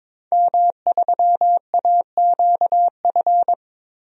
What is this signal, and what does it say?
Morse code, heard as a keyed tone, about 23 wpm, 700 Hz. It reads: M3AQF
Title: DXCC-100 Call Signs - 5 Characters - 22wpm